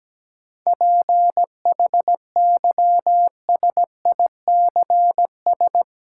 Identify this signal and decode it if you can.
Morse code, heard as a keyed tone, about 17 wpm, 700 Hz. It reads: PHYSICS
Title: Sentences from Top 1000 Words - Review Part 5 - New Word Drill - 17wpm